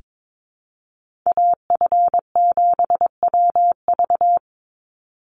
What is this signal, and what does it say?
Morse code, heard as a keyed tone, about 22 wpm, 700 Hz. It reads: AF7WV
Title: USA Call Signs 22wpm